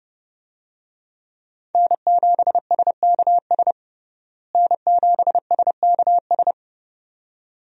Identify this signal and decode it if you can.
Morse code, heard as a keyed tone, about 30 wpm, 700 Hz. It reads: N7SKS N7SKS
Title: USA Call Signs - 5 Characters - Repeated 2x - 30wpm